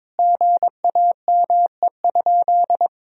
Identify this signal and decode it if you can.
Morse code, heard as a keyed tone, about 22 wpm, 700 Hz. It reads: GAME?
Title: Sentences from Top 1000 Words - Review Part 4 - 22wpm